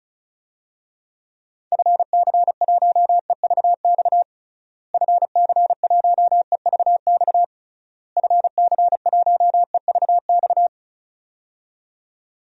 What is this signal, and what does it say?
Morse code, heard as a keyed tone, about 36 wpm, 700 Hz. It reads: FC1EVX FC1EVX FC1EVX
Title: DXCC-100 Call Signs - 6 Characters - Repeated 3x - 35wpm